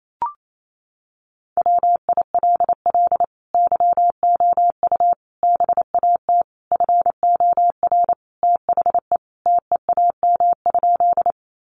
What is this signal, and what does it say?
Morse code, heard as a keyed tone, about 28 wpm, 700 Hz. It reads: WILL YOU BAT FOR THE TEAM?
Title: Sentences from Top 1500 Words 28wpm